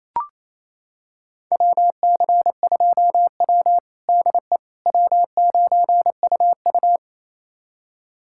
Morse code, heard as a keyed tone, about 28 wpm, 700 Hz. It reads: WC2W DE W9UU
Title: USA Callsign DE Callsign - 4 Characters - 28wpm